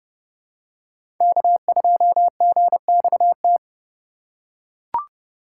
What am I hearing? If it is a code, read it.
Morse code, heard as a keyed tone, about 30 wpm, 700 Hz. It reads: K2GXT
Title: USA Call Signs - 5 Characters - 4x Character Spacing - 30wpm